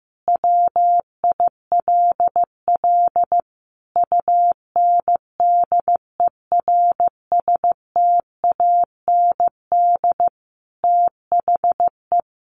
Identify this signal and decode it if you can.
Morse code, heard as a keyed tone, about 15 wpm, 700 Hz. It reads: WILL UNDERSTAND THE
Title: Sentences from Top 1200 Words 15wpm